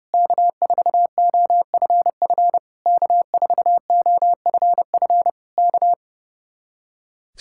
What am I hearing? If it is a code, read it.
Morse code, heard as a keyed tone, about 30 wpm, 700 Hz. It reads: K4OFF K4OFF K
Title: Speed-Racing Common QSO Phrases - 30wpm